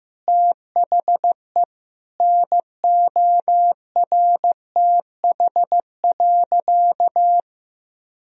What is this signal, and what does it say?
Morse code, heard as a keyed tone, about 16 wpm, 700 Hz. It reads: THE NORTH.
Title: Sentences from Top 1000 Words - Review Part 1 - 15wpm